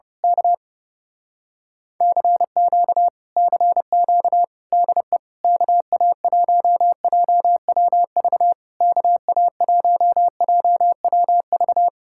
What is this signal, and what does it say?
Morse code, heard as a keyed tone, about 29 wpm, 700 Hz. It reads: K CQ CQ DE KA1JWV KA1JWV
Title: Speed-Racing Common QSO Phrases Encore - 30wpm